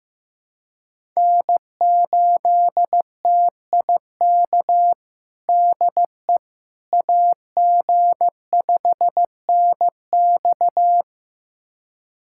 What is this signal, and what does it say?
Morse code, heard as a keyed tone, about 14 wpm, 700 Hz. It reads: N8TIK DE AG5NX
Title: USA Callsign DE Callsign - 5 Characters - 15wpm